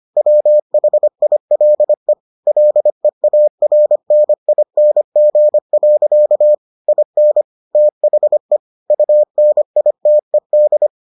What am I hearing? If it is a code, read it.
Morse code, heard as a keyed tone, about 25 wpm, 600 Hz. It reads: WHILE LEARNING. IN THE UNITED